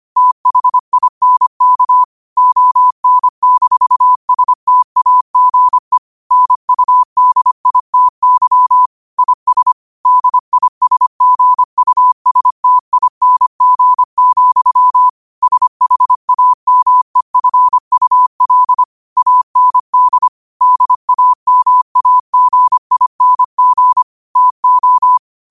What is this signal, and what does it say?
Morse code, heard as a keyed tone, about 25 wpm, 1000 Hz. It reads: THINK ON-STAGE NUDITY IS DISGUSTING, SHAMEFUL AND DAMAGING TO